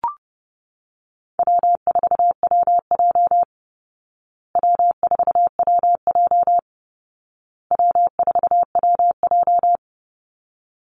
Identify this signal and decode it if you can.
Morse code, heard as a keyed tone, about 30 wpm, 700 Hz. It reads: W4WJ W4WJ W4WJ
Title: Call Signs Repeated 2x - 30wpm